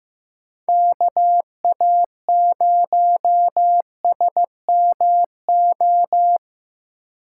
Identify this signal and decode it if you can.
Morse code, heard as a keyed tone, about 15 wpm, 700 Hz. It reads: KA0SMO
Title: USA Call Signs - 6 Characters - 3x Character Spacing - 15wpm